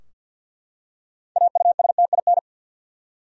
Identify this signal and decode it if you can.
Morse code, heard as a keyed tone, about 51 wpm, 700 Hz. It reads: AUSTIN
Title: USA Common Names - 50wpm